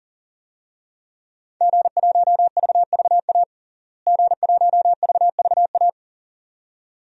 Morse code, heard as a keyed tone, about 40 wpm, 700 Hz. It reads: G1VVA G1VVA
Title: DXCC-100 Call Signs - 5 Characters - Repeated 2x - 40wpm